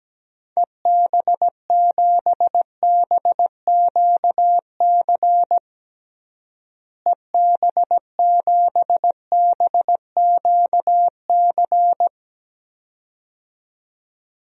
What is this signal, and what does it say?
Morse code, heard as a keyed tone, about 17 wpm, 700 Hz. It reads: EB7BQC EB7BQC
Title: DXCC-100 Call Signs - 6 Characters - Repeated 2x - 17wpm